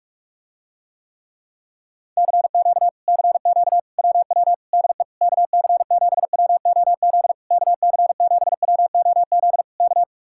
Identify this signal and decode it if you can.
Morse code, heard as a keyed tone, about 44 wpm, 700 Hz. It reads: CQ CQ WW DE KC7WOZ KC7WOZ K